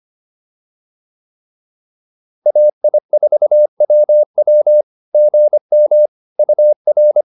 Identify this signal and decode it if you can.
Morse code, heard as a keyed tone, about 25 wpm, 600 Hz. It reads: AI4WW GM UR